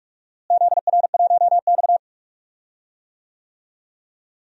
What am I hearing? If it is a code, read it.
Morse code, heard as a keyed tone, about 44 wpm, 700 Hz. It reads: ZR1X
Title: DXCC-100 Call Signs - 4 Characters - Repeated 2x - 45wpm